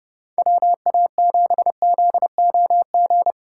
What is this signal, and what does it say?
Morse code, heard as a keyed tone, about 30 wpm, 700 Hz. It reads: WA7ZOG
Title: USA Call Signs - 6 Characters - 4x Character Spacing - 30wpm